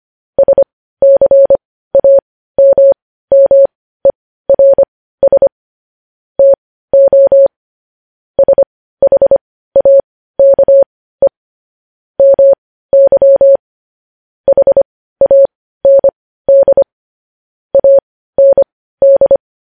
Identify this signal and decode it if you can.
Morse code, heard as a keyed tone, about 25 wpm, spaced as for 15 wpm, 550 Hz. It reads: SCAMMERS TO SHAKE MY HAND AND
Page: news-headlines-in-morse-code-at-15-wpm